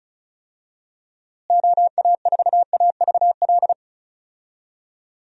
Morse code, heard as a keyed tone, about 34 wpm, 700 Hz. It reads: OA4AVL
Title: USA Call Signs - 6 Characters - Repeated 3x - 35wpm